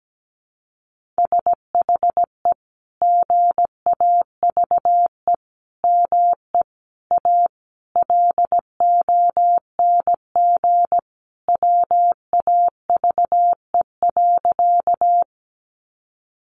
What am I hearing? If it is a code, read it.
Morse code, heard as a keyed tone, about 17 wpm, 700 Hz. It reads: SHE GAVE ME A LONG WAVE.
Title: Sentences from Top 1400 Words 17wpm